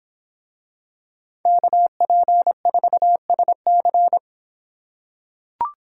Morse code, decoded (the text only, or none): KP4SC